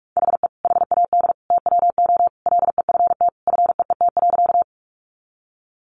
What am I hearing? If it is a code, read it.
Morse code, heard as a keyed tone, about 44 wpm, 700 Hz. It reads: HE HAD TWO LEFT FEET.
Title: Sentences from Top 300 Words 45wpm